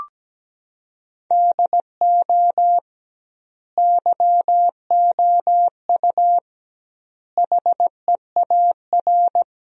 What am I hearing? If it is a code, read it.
Morse code, heard as a keyed tone, about 17 wpm, 700 Hz. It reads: DO YOU HEAR
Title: 2x Word-Spacing - Sentences from Top 300 Words - 17wpm